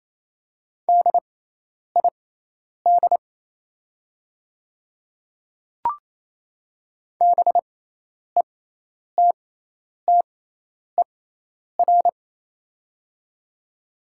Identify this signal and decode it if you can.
Morse code, heard as a keyed tone, about 28 wpm, spaced as for 10 wpm, 700 Hz. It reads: DID BETTER
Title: 6x Character Spacing - Top 500 Words - 28wpm